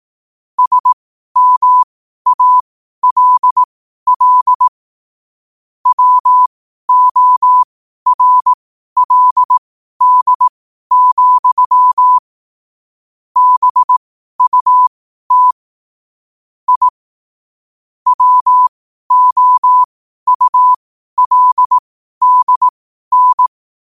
Morse code, heard as a keyed tone, about 18 wpm, spaced as for 12 wpm, 1000 Hz. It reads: SMALL WORLD, BUT I WOULDN
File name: qotdfiles / 12wpm